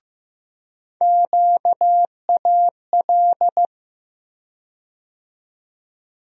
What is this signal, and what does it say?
Morse code, heard as a keyed tone, about 16 wpm, 700 Hz. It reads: QAL
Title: USA Call Sign Postfixes - 3 Letters 15wpm